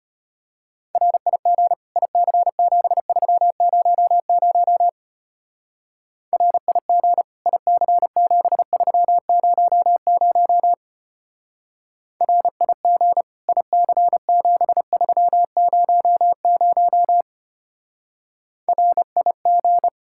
Morse code, heard as a keyed tone, about 32 wpm, 700 Hz. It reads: RIG IC7300 RIG IC7300 RIG IC7300 RIG